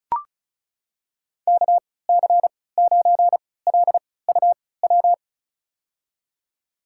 Morse code, heard as a keyed tone, about 36 wpm, spaced as for 20 wpm, 700 Hz. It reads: KC9LUW
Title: USA Call Signs - 6 Characters - 3x Character Spacing - 35wpm